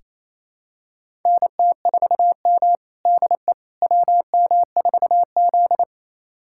Morse code, heard as a keyed tone, about 28 wpm, 700 Hz. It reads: NT4M DE WM4Z
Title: USA Callsign DE Callsign - 4 Characters - 28wpm